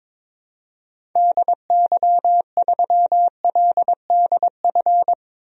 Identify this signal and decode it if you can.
Morse code, heard as a keyed tone, about 22 wpm, 700 Hz. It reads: DY3LDF
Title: DXCC-100 Call Signs - 6 Characters - 22wpm